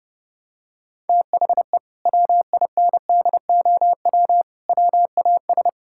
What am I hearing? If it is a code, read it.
Morse code, heard as a keyed tone, about 32 wpm, 700 Hz. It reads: THE WINDOW WAS
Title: Sentences from Top 1600 Words 30wpm